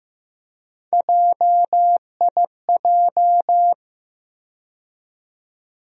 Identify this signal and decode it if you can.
Morse code, heard as a keyed tone, about 15 wpm, 700 Hz. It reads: JIJ
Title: USA Call Sign Postfixes - 3 Letters 15wpm